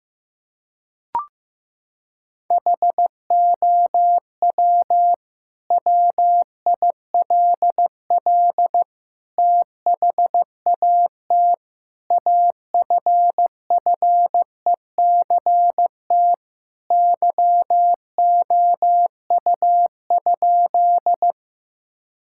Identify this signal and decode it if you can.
Morse code, heard as a keyed tone, about 15 wpm, 700 Hz. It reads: HOW WILL THAT AFFECT YOU?
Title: Sentences from Top 1000 Words - Review Part 6 - 15wpm